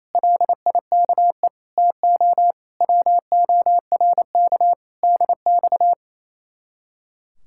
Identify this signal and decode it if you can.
Morse code, heard as a keyed tone, about 33 wpm, 700 Hz. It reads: LIKE TO WORK DX
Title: Speed-Racing Common QSO Phrases - 28wpm